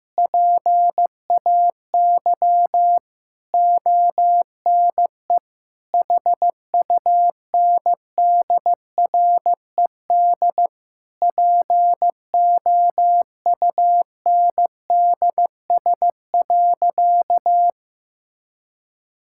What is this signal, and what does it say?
Morse code, heard as a keyed tone, about 16 wpm, 700 Hz. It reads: PAY ONE HUNDRED POUNDS.
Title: Sentences from Top 1000 Words - Review Part 6 - 15wpm